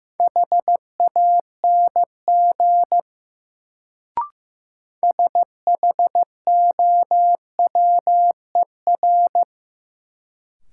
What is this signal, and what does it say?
Morse code, heard as a keyed tone, about 16 wpm, 700 Hz. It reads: HANG SHOWER
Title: Sentences from Top 1600 Words - New Word Drill - 15wpm